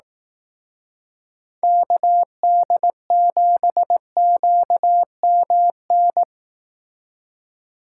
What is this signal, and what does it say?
Morse code, heard as a keyed tone, about 18 wpm, 700 Hz. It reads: KD7QMN
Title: Speed Racing - USA 2x3 Call Signs 15wpm